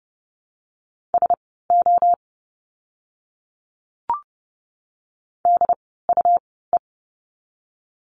Morse code, heard as a keyed tone, about 30 wpm, spaced as for 17 wpm, 700 Hz. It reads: SO DUE
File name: Lesson 17f - Letter u - Words 30wpm